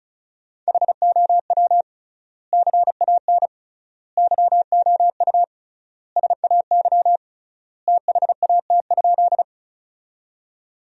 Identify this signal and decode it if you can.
Morse code, heard as a keyed tone, about 34 wpm, 700 Hz. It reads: HOW CAN YOU SAY THAT?
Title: 3x Word-Spacing - Sentences from Top 200 Words - 35wpm